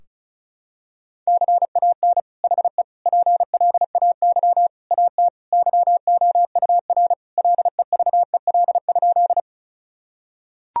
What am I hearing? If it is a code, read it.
Morse code, heard as a keyed tone, about 36 wpm, 700 Hz. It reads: CAN HE PLAY AT YOUR LEVEL?
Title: Sentences from Top 1000 Words - Review Part 4 - 35wpm